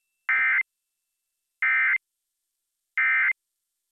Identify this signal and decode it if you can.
Emergency Alert System (SAME) tones, heard as harsh buzzing bursts. It is heard from 0.3 to 3.3 s.